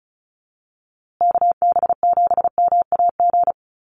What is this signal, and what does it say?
Morse code, heard as a keyed tone, about 34 wpm, 700 Hz. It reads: KB7MAG
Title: USA Call Signs Repeated 3x 35wpm